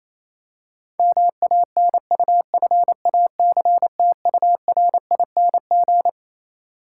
Morse code, heard as a keyed tone, about 27 wpm, 700 Hz. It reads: MANUFACTURING
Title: K1USN SST QSO Practice 28wpm